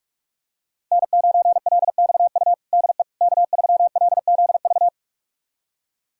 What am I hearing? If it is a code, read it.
Morse code, heard as a keyed tone, about 41 wpm, 700 Hz. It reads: N9LXU DE K3LZV
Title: USA Callsign DE Callsign - 5 Characters - 45wpm